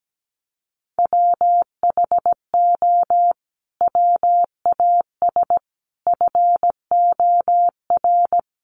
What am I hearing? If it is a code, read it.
Morse code, heard as a keyed tone, about 17 wpm, 700 Hz. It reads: WHO WAS FOR
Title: Sentences from Top 1400 Words 17wpm